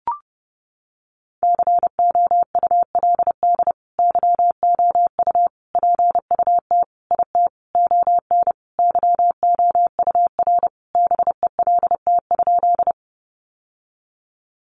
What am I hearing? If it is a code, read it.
Morse code, heard as a keyed tone, about 30 wpm, 700 Hz. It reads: COULD YOU PUT IT ON YOUR BELT?
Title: Sentences from Top 1400 Words 30wpm